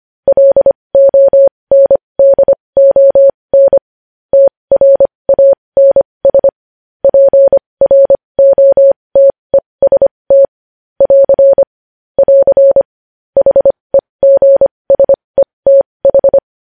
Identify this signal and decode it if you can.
Morse code, heard as a keyed tone, about 25 wpm, spaced as for 20 wpm, 550 Hz. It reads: LONDON TRANS PROTEST <AR> <AR> HEGSETH
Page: News Headlines in Morse Code at 20 WPM